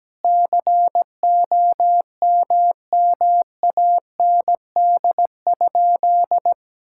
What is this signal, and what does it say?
Morse code, heard as a keyed tone, about 17 wpm, 700 Hz. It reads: COMMAND?
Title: Sentences from Top 1600 Words 17wpm